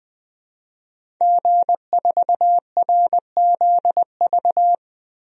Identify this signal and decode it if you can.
Morse code, heard as a keyed tone, about 20 wpm, 700 Hz. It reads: G4RZV
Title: DXCC-100 Call Signs - 5 Characters - Repeated 3x - 20wpm